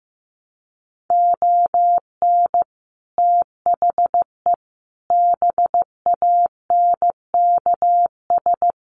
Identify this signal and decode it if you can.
Morse code, heard as a keyed tone, about 15 wpm, 700 Hz. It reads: ON THE BANKS
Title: Sets of 5 Words Encore2 15wpm